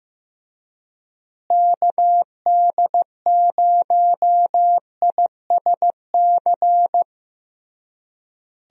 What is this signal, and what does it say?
Morse code, heard as a keyed tone, about 15 wpm, 700 Hz. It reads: KD0ISC
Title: USA Call Signs - 6 Characters - Repeated 3x - 15wpm